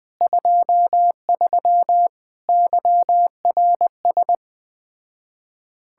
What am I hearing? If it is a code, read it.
Morse code, heard as a keyed tone, about 20 wpm, 700 Hz. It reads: 23 YRS
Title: Speed-Racing Common QSO Phrases - 17wpm